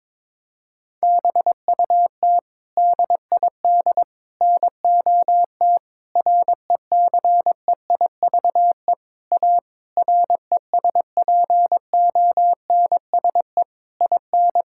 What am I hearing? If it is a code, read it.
Morse code, heard as a keyed tone, about 22 wpm, 700 Hz. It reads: BUT DID NOT RECEIVE A RESPONSE IN
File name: practice / 22wpm / Sets of 8 Words - 22wpm